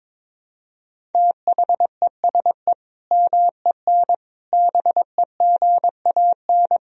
Morse code, heard as a keyed tone, about 22 wpm, 700 Hz. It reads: THESE MEN BEGAN
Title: Sentences from Top 1000 Words - Review Part 1 - 22wpm